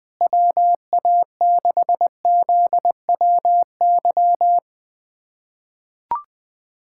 Morse code, heard as a keyed tone, about 20 wpm, 700 Hz. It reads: WA6ZWY
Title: USA Callsign DE Callsign - 6 Characters - 20wpm